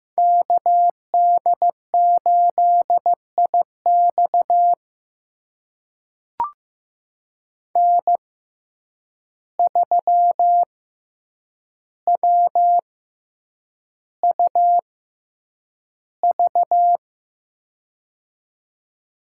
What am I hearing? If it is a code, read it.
Morse code, heard as a keyed tone, about 15 wpm, spaced as for 8 wpm, 700 Hz. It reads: KD8IX N3WUV
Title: USA Call Signs - 5 Characters - 6x Character Spacing - 15wpm